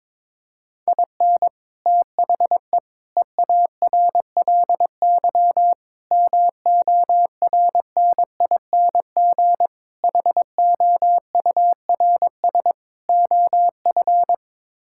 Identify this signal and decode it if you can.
Morse code, heard as a keyed tone, about 22 wpm, 700 Hz. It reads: IN THE EARLY MORNING HOURS OF
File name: practice / 22wpm / Sets of 6 Words - 22wpm